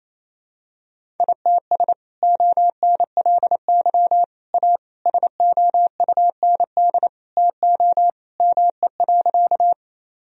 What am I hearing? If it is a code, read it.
Morse code, heard as a keyed tone, about 28 wpm, 700 Hz. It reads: ITS ONLY A SOUND TO ME.
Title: Sentences from Top 1000 Words - Review Part 1 - 28wpm